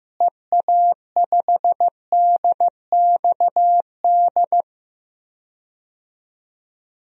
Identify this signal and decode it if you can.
Morse code, heard as a keyed tone, about 15 wpm, 700 Hz. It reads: EA5DXD
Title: DXCC-100 Call Signs - 6 Characters - Repeated 3x - 15wpm